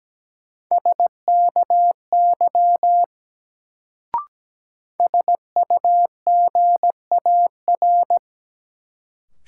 Morse code, heard as a keyed tone, about 17 wpm, 700 Hz. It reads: SKY SUGAR
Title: Sentences from Top 1200 Words - New Word Drill - 17wpm